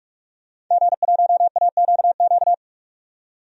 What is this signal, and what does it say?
Morse code, heard as a keyed tone, about 45 wpm, 700 Hz. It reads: G1AQQ
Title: DXCC-100 Call Signs - 5 Characters - 45wpm